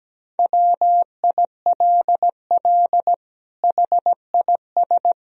Morse code, heard as a keyed tone, about 17 wpm, 700 Hz. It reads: WILL HIS
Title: Sentences from Top 1000 Words - Review Part 4 - 17wpm